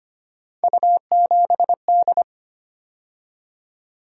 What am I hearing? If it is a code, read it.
Morse code, heard as a keyed tone, about 25 wpm, 700 Hz. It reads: U7D
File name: Three Letter-Number Combinations - 25wpm